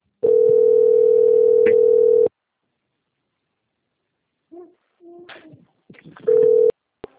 Ringback tone (North America), 440 + 480 Hz.